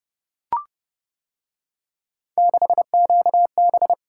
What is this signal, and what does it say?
Morse code, heard as a keyed tone, about 29 wpm, 700 Hz. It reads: 6QB